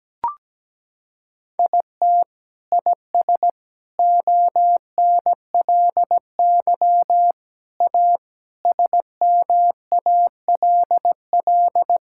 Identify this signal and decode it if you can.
Morse code, heard as a keyed tone, about 17 wpm, 700 Hz. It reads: IT IS ONLY A SMALL
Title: Sentences from Top 1000 Words - Review Part 5 - 17wpm